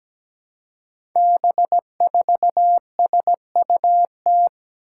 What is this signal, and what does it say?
Morse code, heard as a keyed tone, about 17 wpm, 700 Hz. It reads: B4SUT
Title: Five Challenging Letter-Number Combinations 17wpm